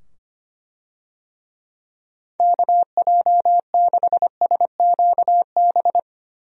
Morse code, heard as a keyed tone, about 25 wpm, 700 Hz. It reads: KJ6SQB
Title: USA Call Signs - 6 Characters - 25wpm